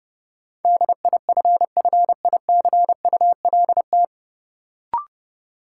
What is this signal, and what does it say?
Morse code, heard as a keyed tone, about 32 wpm, 700 Hz. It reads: DIFFICULT
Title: Sets of 4 Words - New Word Drill - 30wpm